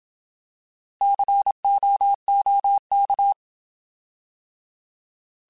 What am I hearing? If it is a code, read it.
Morse code, heard as a keyed tone, about 27 wpm, 800 Hz. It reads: COOK
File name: Alphabetical / Alphabetical mp3s / Cs @25